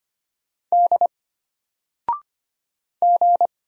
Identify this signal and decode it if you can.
Morse code, heard as a keyed tone, about 25 wpm, 700 Hz. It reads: D G
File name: Single-Letter-Number-25wpm